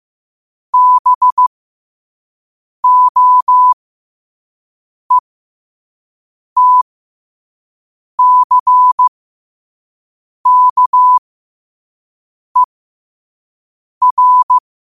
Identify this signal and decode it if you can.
Morse code, heard as a keyed tone, about 15 wpm, spaced as for 5 wpm, 1000 Hz. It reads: BOETCKER
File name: qotdfiles / 5wpm